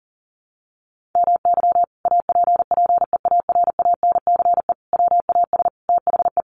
Morse code, heard as a keyed tone, about 40 wpm, 700 Hz. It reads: MY APPEARANCE WAS THE
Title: Sentences from Top 1100 Words 40wpm